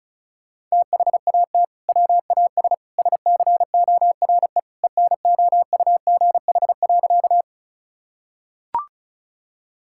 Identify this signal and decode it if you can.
Morse code, heard as a keyed tone, about 34 wpm, 700 Hz. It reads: THAT WAS SCORE ENOUGH.
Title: Sentences from Top 1600 Words 35wpm